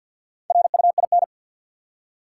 Morse code, heard as a keyed tone, about 50 wpm, 700 Hz. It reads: RUIN
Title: Four Letter Words - 50wpm